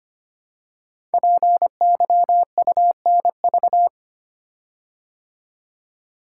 Morse code, heard as a keyed tone, about 25 wpm, 700 Hz. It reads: PYUNV
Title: Code Groups - 5 Letters - 25wpm